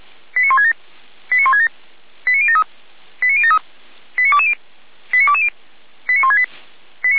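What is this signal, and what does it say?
Five-tone selective call (ZVEI), heard as quick runs of five tones, tones of about 70 ms. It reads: 89167 89167 89073 89073 89220 89220 89168
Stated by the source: ZVEI five-tone selective call 89060 received by SDR.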